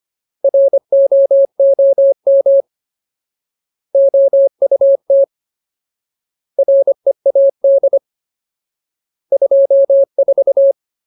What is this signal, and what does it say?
Morse code, heard as a keyed tone, about 25 wpm, 550 Hz. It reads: ROOM OUT READ 24
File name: Ultimate Word List / 25WPM Ultimate